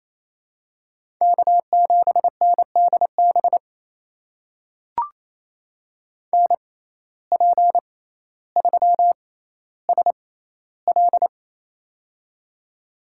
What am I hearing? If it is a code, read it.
Morse code, heard as a keyed tone, about 28 wpm, spaced as for 14 wpm, 700 Hz. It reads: K7NDB NP3SL
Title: USA Call Signs - 5 Characters - 6x Character Spacing - 28wpm